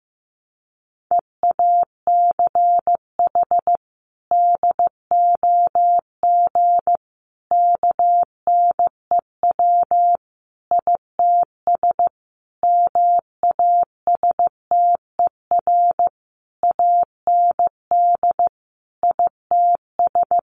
Morse code, heard as a keyed tone, about 14 wpm, 700 Hz. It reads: EACH DOG KNEW ITS MASTER AND ITS
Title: Sentences from Top 1300 Words 15wpm